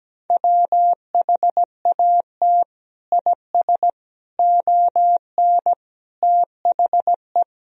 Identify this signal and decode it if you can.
Morse code, heard as a keyed tone, about 17 wpm, 700 Hz. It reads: WHAT IS ON THE
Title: Sentences from Top 1000 Words - Review Part 2 - 17wpm